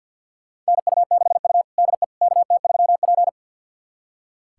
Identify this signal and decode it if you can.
Morse code, heard as a keyed tone, about 46 wpm, 700 Hz. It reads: NU6U DE KT3P
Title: USA Callsign DE Callsign - 4 Characters - 50wpm